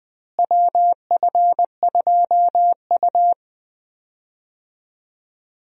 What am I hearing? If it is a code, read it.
Morse code, heard as a keyed tone, about 20 wpm, 690 Hz. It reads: WF2U
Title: USA Call Signs - 4 Characters - Repeated 3x - 20wpm